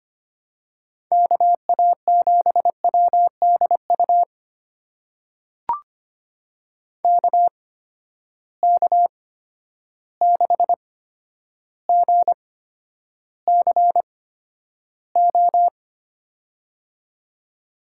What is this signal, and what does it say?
Morse code, heard as a keyed tone, about 25 wpm, spaced as for 11 wpm, 700 Hz. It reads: KA7WDU KK6GCO
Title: USA Call Signs - 6 Characters - 8x Character Spacing - 25wpm